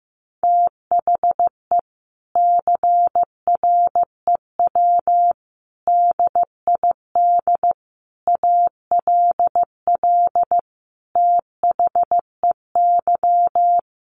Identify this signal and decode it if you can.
Morse code, heard as a keyed tone, about 15 wpm, 700 Hz. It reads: THE CREW DID ALL THEY
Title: Sentences from Top 1500 Words 15wpm